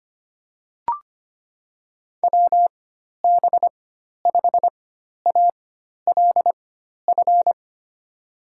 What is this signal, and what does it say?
Morse code, heard as a keyed tone, about 25 wpm, spaced as for 12 wpm, 700 Hz. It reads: WB5ALF
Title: USA Call Signs - 6 Characters - 4x Character Spacing - 25wpm